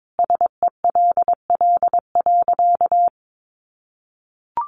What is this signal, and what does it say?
Morse code, heard as a keyed tone, about 22 wpm, 700 Hz. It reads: SELL.
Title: Sentences from Top 1300 Words 22wpm